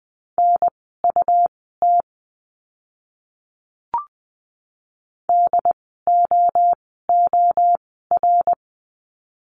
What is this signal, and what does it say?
Morse code, heard as a keyed tone, about 20 wpm, spaced as for 15 wpm, 700 Hz. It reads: NUT DOOR
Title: 101 Lesson 17f - Letter u - Words 20wpm